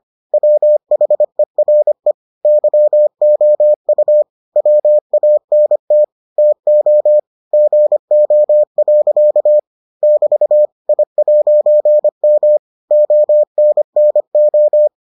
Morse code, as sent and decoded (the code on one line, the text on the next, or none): .-- .... .-. / -.-- --- ..- / .-- .- -. - / - --- / --. --- .-.-.- / -...- / .. .----. -- / --- -. -. ---
WHERE YOU WANT TO GO. <BT> I'M ONNO